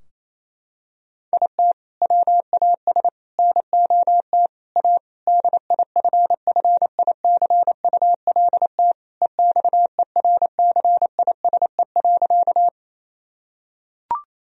Morse code, heard as a keyed tone, about 28 wpm, 700 Hz. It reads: IT WAS NOT A DIFFICULT EXERCISE.
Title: Sentences from Top 1000 Words - Review Part 4 - 28wpm